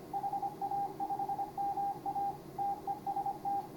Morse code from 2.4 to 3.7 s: TEST